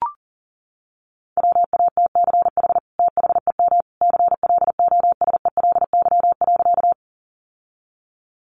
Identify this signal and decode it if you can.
Morse code, heard as a keyed tone, about 40 wpm, 700 Hz. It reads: WATCH THEM CLOSELY.
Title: Sentences from Top 1400 Words 40wpm